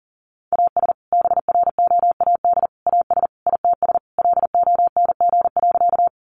Morse code, heard as a keyed tone, about 37 wpm, 700 Hz. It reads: AS BROAD AS ITS LONG.
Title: Sentences from Top 1000 Words 40wpm